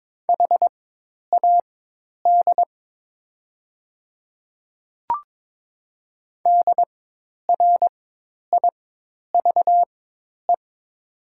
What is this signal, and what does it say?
Morse code, heard as a keyed tone, about 22 wpm, spaced as for 10 wpm, 700 Hz. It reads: HAD DRIVE
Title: 4x Character Spacing - Top 500 Words - 22wpm